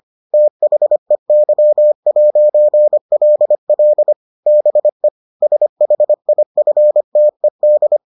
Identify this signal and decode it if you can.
Morse code, heard as a keyed tone, about 25 wpm, 600 Hz. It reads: THEY'LL BE SHIFTED